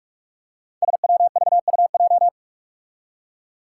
Morse code, heard as a keyed tone, about 44 wpm, 700 Hz. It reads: SWVUJ
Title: Five Challenging Letter-Number Combinations 45wpm